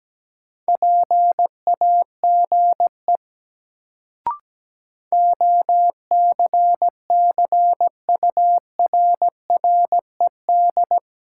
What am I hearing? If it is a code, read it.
Morse code, heard as a keyed tone, about 17 wpm, 700 Hz. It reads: PAGE OCCURRED
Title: Sets of 5 Words Encore - New Word Drill - 17wpm